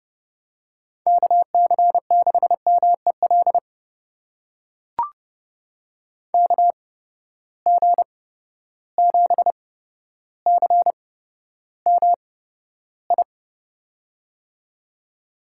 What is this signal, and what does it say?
Morse code, heard as a keyed tone, about 30 wpm, spaced as for 13 wpm, 700 Hz. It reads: KC6MEL KG7CMI
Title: USA Call Signs - 6 Characters - 8x Character Spacing - 30wpm